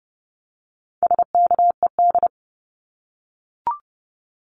Morse code, heard as a keyed tone, about 30 wpm, 700 Hz. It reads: SKED
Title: Abbreviations 30wpm